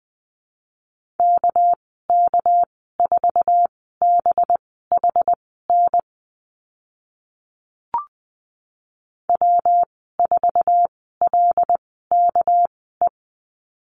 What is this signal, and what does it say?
Morse code, heard as a keyed tone, about 20 wpm, spaced as for 15 wpm, 700 Hz. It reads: KK4BHN W4LKE
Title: 200 Lesson 32g - Letter k - Call Signs 20wpm